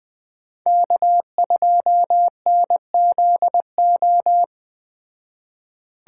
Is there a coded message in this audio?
Yes: Morse code from 0.7 to 4.4 s, heard as a keyed tone, about 20 wpm, 700 Hz. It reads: K2NZO